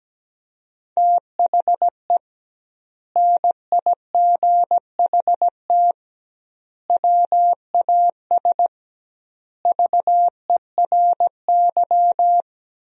Morse code, heard as a keyed tone, about 17 wpm, 700 Hz. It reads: THE NIGHT WAS VERY
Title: 2x Word-Spacing - Sentences from Top 500 Words - 17wpm